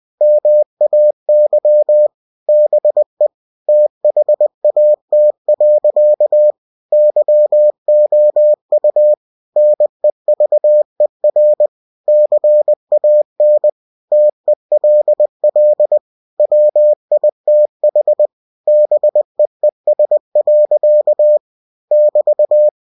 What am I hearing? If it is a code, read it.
Morse code, heard as a keyed tone, about 20 wpm, 600 Hz. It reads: MAY BE THAT. YOU NEVER CAN TELL WITH BEES. <BT>
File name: cw-books / winnie-the-pooh / winnie-20wpm-page-0007